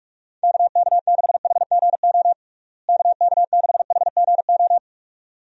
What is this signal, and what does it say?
Morse code, heard as a keyed tone, about 46 wpm, 700 Hz. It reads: KK6HGO KK6HGO
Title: USA Call Signs - 6 Characters - Repeated 2x - 45wpm